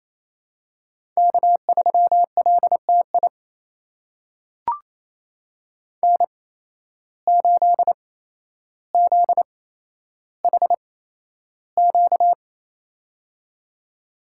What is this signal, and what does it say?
Morse code, heard as a keyed tone, about 27 wpm, spaced as for 12 wpm, 700 Hz. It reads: K3LTI N8ZHQ
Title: USA Call Signs - 5 Characters - 8x Character Spacing - 28wpm